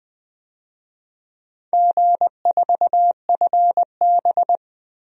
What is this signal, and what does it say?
Morse code, heard as a keyed tone, about 21 wpm, 700 Hz. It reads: G4FB
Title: DXCC-100 Call Signs - 4 Characters - Repeated 2x - 20wpm